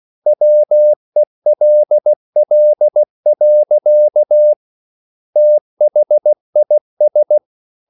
Morse code, heard as a keyed tone, about 16 wpm, 600 Hz. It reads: WELL. THIS